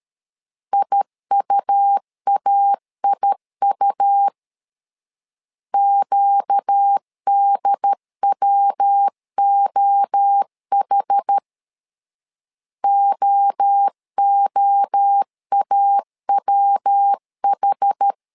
Morse code, heard as a keyed tone, about 13 wpm, 800 Hz. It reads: IUAIU QDWOH OOAWH
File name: LEZIONI / 4 Q 12wpm 1833